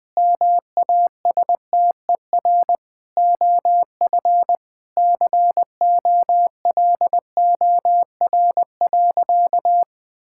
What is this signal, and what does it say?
Morse code, heard as a keyed tone, about 21 wpm, 700 Hz. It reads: MASTER OF COLOR.
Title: Sentences with 6-Word Phrases - 20wpm